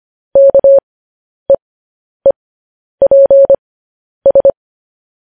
Morse code, heard as a keyed tone, about 25 wpm, spaced as for 10 wpm, 550 Hz. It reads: KEEPS
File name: podcast / News headlines-1x-25x10